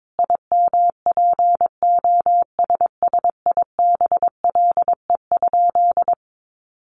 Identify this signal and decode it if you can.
Morse code, heard as a keyed tone, about 22 wpm, 700 Hz. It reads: IMPOSSIBLE?
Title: Sentences from Top 1100 Words 22wpm